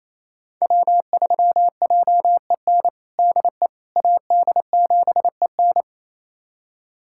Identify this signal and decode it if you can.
Morse code, heard as a keyed tone, about 28 wpm, 700 Hz. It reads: W3JEN DE AD7EN
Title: USA Callsign DE Callsign - 5 Characters - 28wpm